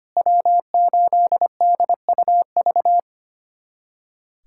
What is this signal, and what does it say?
Morse code, heard as a keyed tone, about 25 wpm, 700 Hz. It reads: W8DUV
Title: USA Callsign DE Callsign 25wpm